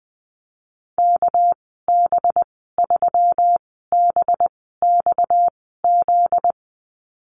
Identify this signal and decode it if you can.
Morse code, heard as a keyed tone, about 20 wpm, spaced as for 14 wpm, 700 Hz. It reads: KB3BXZ
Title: 245 Lesson 39g - Letter z - Call Signs 20wpm